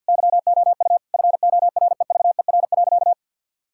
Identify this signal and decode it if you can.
Morse code, heard as a keyed tone, about 50 wpm, 700 Hz. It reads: YOU FOREVER.
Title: Sentences from Top 1600 Words 50wpm